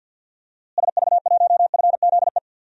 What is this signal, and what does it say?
Morse code, heard as a keyed tone, about 52 wpm, 700 Hz. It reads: SV1FZE
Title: DXCC-100 Call Signs - 6 Characters - Repeated 3x - 50wpm